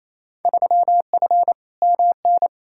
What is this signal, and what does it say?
Morse code, heard as a keyed tone, about 28 wpm, 700 Hz. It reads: 3F MN
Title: Field Day Exchanges Encore - 28wpm